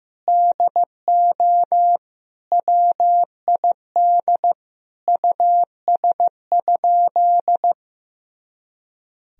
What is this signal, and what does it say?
Morse code, heard as a keyed tone, about 15 wpm, 700 Hz. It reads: DO WID US?
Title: Sentences from Top 200 Words and Abbreviations 15wpm